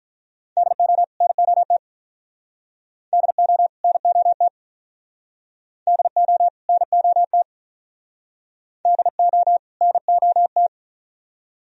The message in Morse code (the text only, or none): DO NOT DO NOT DO NOT DO NOT